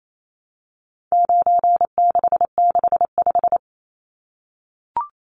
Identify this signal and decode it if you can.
Morse code, heard as a keyed tone, about 28 wpm, 700 Hz. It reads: 9665
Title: Numbers - 4-Digit 28wpm